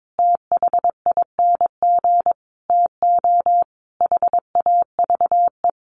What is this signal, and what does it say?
Morse code, heard as a keyed tone, about 22 wpm, 700 Hz. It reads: THING TO HAVE
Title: Sentences from Top 1000 Words 22wpm